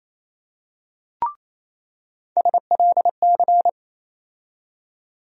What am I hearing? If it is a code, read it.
Morse code, heard as a keyed tone, about 28 wpm, 700 Hz. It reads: SLC
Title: USA Call Sign Postfixes - 3 Letters 28wpm